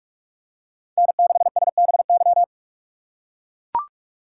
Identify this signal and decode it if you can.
Morse code, heard as a keyed tone, about 44 wpm, 700 Hz. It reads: N6SBY
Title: USA Call Signs - 5 Characters - 6x Character Spacing - 45wpm